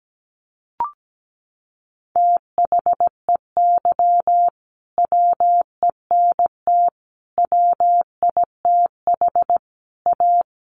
Morse code, heard as a keyed tone, about 17 wpm, 700 Hz. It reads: THEY WENT WITH A
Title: Sentences from Top 800 Words 17wpm